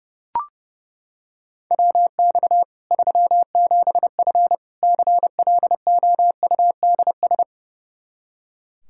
Morse code, heard as a keyed tone, about 32 wpm, 700 Hz. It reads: WX 37F CLOUDS
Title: Practice QSO - Exchange 2 - Info - 30wpm